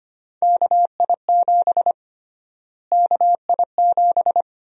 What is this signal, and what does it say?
Morse code, heard as a keyed tone, about 25 wpm, 700 Hz. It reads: KI7 KI7
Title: USA Call Sign Prefixes - Repeated 2x - 25wpm